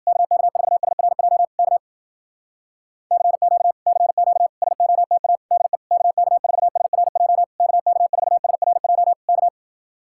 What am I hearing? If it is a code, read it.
Morse code, heard as a keyed tone, about 58 wpm, 700 Hz. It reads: KK4SRJ K CQ CQ SOTA DE KK4SRJ KK4SRJ K